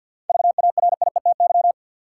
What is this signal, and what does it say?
Morse code, heard as a keyed tone, about 53 wpm, 700 Hz. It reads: VARIETY
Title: Seven Letter Words - 50wpm